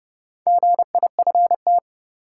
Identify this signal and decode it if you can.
Morse code, heard as a keyed tone, about 30 wpm, 700 Hz. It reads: GIFT
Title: Sentences from Top 900 Words - New Word Drill - 30wpm